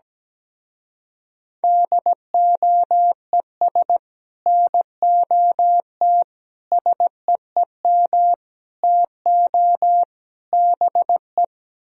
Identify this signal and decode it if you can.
Morse code, heard as a keyed tone, about 17 wpm, 700 Hz. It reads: DOES NOT SEEM TO BE
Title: Speed-Racing - Sets of 5 Words 17wpm